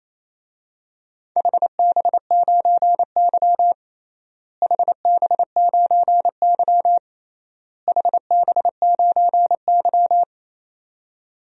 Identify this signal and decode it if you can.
Morse code, heard as a keyed tone, about 28 wpm, 700 Hz. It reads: HB9Y HB9Y HB9Y
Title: DXCC-100 Call Signs - 4 Characters - Repeated 3x - 28wpm